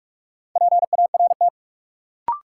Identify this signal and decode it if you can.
Morse code, heard as a keyed tone, about 45 wpm, 700 Hz. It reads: PART
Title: Sentences from Top 200 Words - New Word Drill - 45wpm